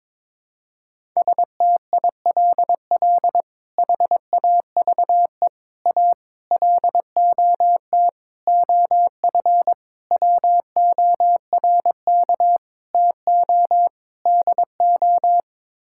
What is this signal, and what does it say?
Morse code, heard as a keyed tone, about 21 wpm, 700 Hz. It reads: STILL HAVE A LOT OF WORK TO DO
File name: practice / 22wpm / Sets of 8 Words - 22wpm